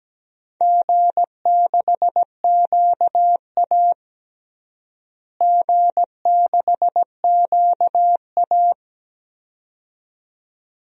Morse code, heard as a keyed tone, about 17 wpm, 700 Hz. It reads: G6QA G6QA
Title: DXCC-100 Call Signs - 4 Characters - Repeated 2x - 17wpm